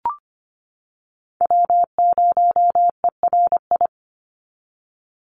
Morse code, heard as a keyed tone, about 25 wpm, 700 Hz. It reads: W0ERI
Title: NA SOTA Chaser Callsigns 25wpm